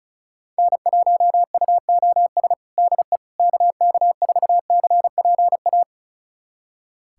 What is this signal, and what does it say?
Morse code, heard as a keyed tone, about 36 wpm, 700 Hz. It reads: N1UOS DE KK4CPA
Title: USA Callsign DE Callsign 35wpm